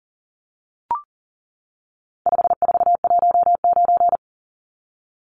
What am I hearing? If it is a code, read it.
Morse code, heard as a keyed tone, about 40 wpm, 700 Hz. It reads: 5419
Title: Numbers - 4-Digit 40wpm